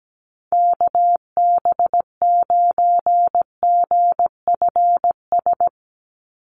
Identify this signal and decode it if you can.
Morse code, heard as a keyed tone, about 17 wpm, 700 Hz. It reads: KB9GFS
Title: USA Call Signs 17wpm